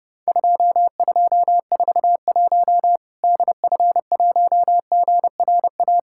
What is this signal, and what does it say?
Morse code, heard as a keyed tone, about 30 wpm, 700 Hz. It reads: 2241 DF1GRA
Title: Worked All Europe DX Contest - QTC Exchange - 30wpm